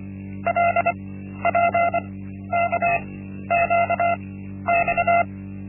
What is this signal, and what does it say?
Morse code, heard as a keyed tone, about 24 wpm, spaced as for 12 wpm, 690 Hz. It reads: LPKQX
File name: Linksmakalnis-1